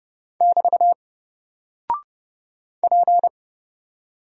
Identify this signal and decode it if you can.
Morse code, heard as a keyed tone, about 30 wpm, 700 Hz. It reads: <BT> P
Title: Single Character (Letter:Number:Punctuation:Prosign) Alt 30wpm